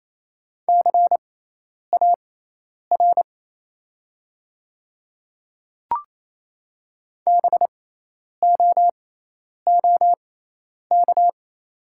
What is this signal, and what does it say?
Morse code, heard as a keyed tone, about 28 wpm, spaced as for 10 wpm, 700 Hz. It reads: CAR BOOK
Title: 6x Character Spacing - Top 500 Words - 28wpm